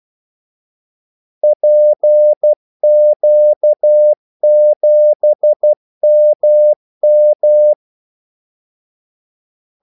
Morse code, heard as a keyed tone, about 12 wpm, 600 Hz. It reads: PQ7MM
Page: 2 ekstra ordmellemrum mellem kaldesignalerne